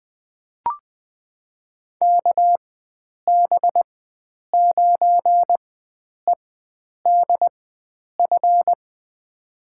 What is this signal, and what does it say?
Morse code, heard as a keyed tone, about 20 wpm, spaced as for 9 wpm, 700 Hz. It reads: KB9EDF
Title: USA Call Signs - 4x Character Spacing - 20wpm